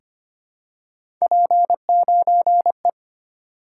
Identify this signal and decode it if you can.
Morse code, heard as a keyed tone, about 25 wpm, 700 Hz. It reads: P9E
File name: Three Letter-Number Combinations - 25wpm